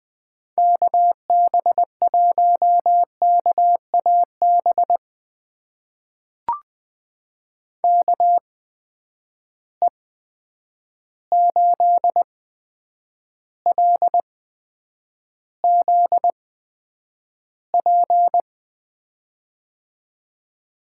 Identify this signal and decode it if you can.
Morse code, heard as a keyed tone, about 20 wpm, spaced as for 9 wpm, 700 Hz. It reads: KB1KAB KE8LZP
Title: USA Call Signs - 6 Characters - 8x Character Spacing - 20wpm